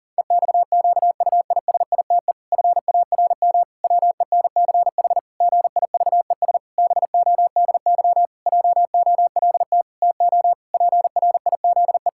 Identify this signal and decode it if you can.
Morse code, heard as a keyed tone, about 40 wpm, 700 Hz. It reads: EXQUISITE FARM WENCH GIVES BODY JOLT TO PRIZE
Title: Pangram Challenge v2 - 40wpm